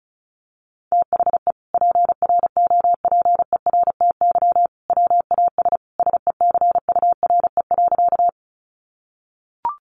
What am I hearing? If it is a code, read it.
Morse code, heard as a keyed tone, about 34 wpm, 700 Hz. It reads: THE PROPERTY WAS SECURE.
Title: Sentences from Top 1200 Words 35wpm